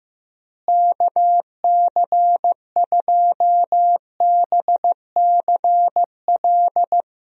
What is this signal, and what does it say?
Morse code, heard as a keyed tone, about 15 wpm, 700 Hz. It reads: KC2BCL
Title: USA Call Signs - 6 Characters - Repeated 2x - 15wpm